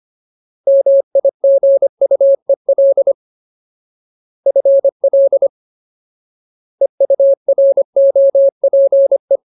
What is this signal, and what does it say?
Morse code, heard as a keyed tone, about 25 wpm, 550 Hz. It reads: MIGUEL FL EUROPE
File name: Ultimate Word List / 25WPM Ultimate